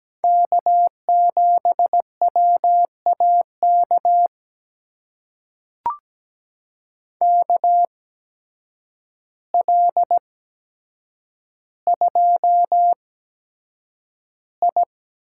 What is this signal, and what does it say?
Morse code, heard as a keyed tone, about 17 wpm, spaced as for 8 wpm, 700 Hz. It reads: K7WAK KL2I
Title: USA Call Signs - 5 Characters - 8x Character Spacing - 17wpm